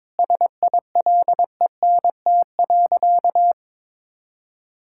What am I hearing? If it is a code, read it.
Morse code, heard as a keyed tone, about 22 wpm, 700 Hz. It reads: SILENT.
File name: Sentences with 7-Word Phrases - 22wpm